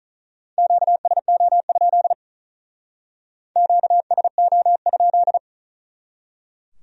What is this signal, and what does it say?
Morse code, heard as a keyed tone, about 45 wpm, 700 Hz. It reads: QSO? QSO?